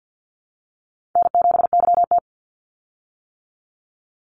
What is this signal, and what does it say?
Morse code, heard as a keyed tone, about 50 wpm, 700 Hz. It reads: N7YT
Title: Easy Call Signs - 50wpm